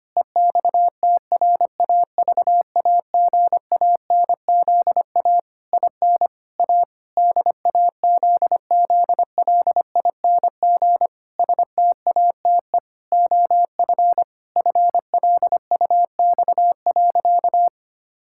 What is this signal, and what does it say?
Morse code, heard as a keyed tone, about 25 wpm, 700 Hz. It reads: EXTRAVAGANZA IN A DAZZLING STATE OF FLUX.
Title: Pangram Challenge v2 - 25wpm